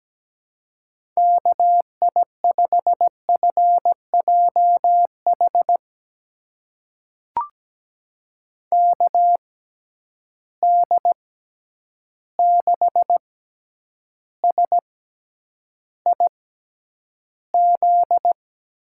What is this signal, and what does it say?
Morse code, heard as a keyed tone, about 17 wpm, spaced as for 9 wpm, 700 Hz. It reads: KI5FJH KD6SIZ
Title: USA Call Signs - 6 Characters - 6x Character Spacing - 17wpm